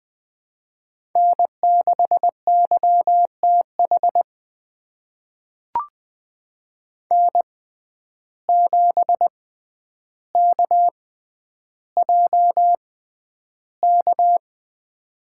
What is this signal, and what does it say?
Morse code, heard as a keyed tone, about 20 wpm, spaced as for 10 wpm, 700 Hz. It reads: N6YTH N7KJK
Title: USA Call Signs - 5 Characters - 6x Character Spacing - 20wpm